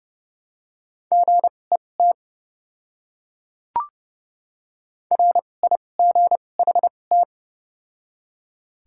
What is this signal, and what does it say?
Morse code, heard as a keyed tone, about 30 wpm, spaced as for 22 wpm, 700 Hz. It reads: GET RIGHT
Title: 2x Character Spacing - Top 300 Words - 30wpm